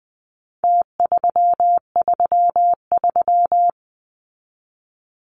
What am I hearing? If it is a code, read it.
Morse code, heard as a keyed tone, about 20 wpm, 700 Hz. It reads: T333
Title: Numbers - Contest Style - 4-Digit 20wpm